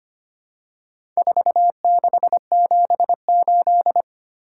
Morse code, heard as a keyed tone, about 25 wpm, 700 Hz. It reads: 4678
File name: practice / 25wpm / Four Letter-Number Combinations - 25wpm